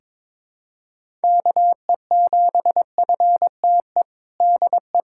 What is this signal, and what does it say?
Morse code, heard as a keyed tone, about 22 wpm, 700 Hz. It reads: KE7FTE DE
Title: USA Callsign DE Callsign - 6 Characters - 22wpm